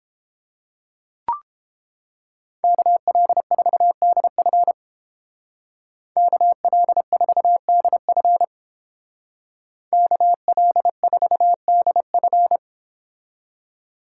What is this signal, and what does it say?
Morse code, heard as a keyed tone, about 31 wpm, 700 Hz. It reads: KL4DF KL4DF KL4DF